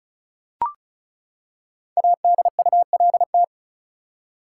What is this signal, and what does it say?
Morse code, heard as a keyed tone, about 34 wpm, 700 Hz. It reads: ADULT
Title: Five Letter Words - 35wpm